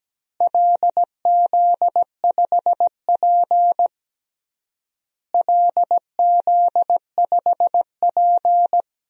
Morse code, heard as a keyed tone, about 17 wpm, 700 Hz. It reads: LZ5P LZ5P
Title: DXCC-100 Call Signs - 4 Characters - Repeated 2x - 17wpm